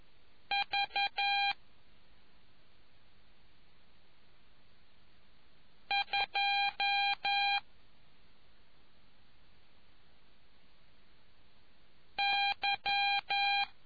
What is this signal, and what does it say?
Morse code, heard as a keyed tone, about 11 wpm, 800 Hz. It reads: V 2 Y